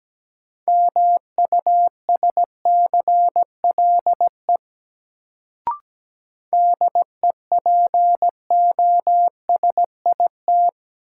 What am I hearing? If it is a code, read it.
Morse code, heard as a keyed tone, about 17 wpm, 700 Hz. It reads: MUSCLE DEPOSIT
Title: Sentences from Top 1300 Words - New Word Drill - 17wpm